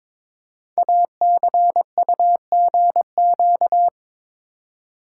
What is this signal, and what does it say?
Morse code, heard as a keyed tone, about 22 wpm, 690 Hz. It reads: ACUGQ
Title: Code Groups - 5 Letters - 22wpm